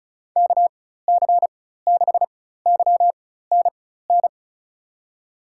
Morse code, heard as a keyed tone, about 36 wpm, spaced as for 16 wpm, 700 Hz. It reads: KC6YNN
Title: USA Call Signs - 4x Character Spacing - 35wpm